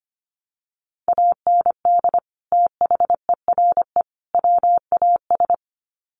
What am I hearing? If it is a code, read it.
Morse code, heard as a keyed tone, about 25 wpm, 700 Hz. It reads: AND THERE WAS
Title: Sets of 3 Words 25wpm